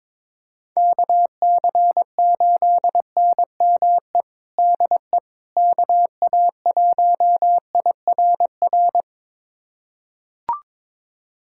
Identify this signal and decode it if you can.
Morse code, heard as a keyed tone, about 22 wpm, 700 Hz. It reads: KC8NME DE KA1IRR
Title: USA Callsign DE Callsign - 6 Characters - 22wpm